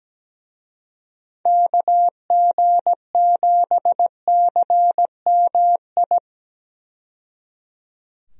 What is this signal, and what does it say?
Morse code, heard as a keyed tone, about 17 wpm, 700 Hz. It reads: KG7CMI
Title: USA Call Signs - 6 Characters - 17wpm